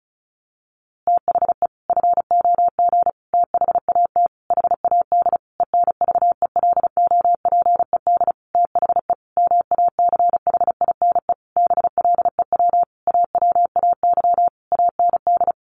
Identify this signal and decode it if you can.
Morse code, heard as a keyed tone, about 34 wpm, 700 Hz. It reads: THE FOG THAT HAD ENVELOPED THE MACHINE BLEW AWAY AND
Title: A Sound Of Thunder 35wpm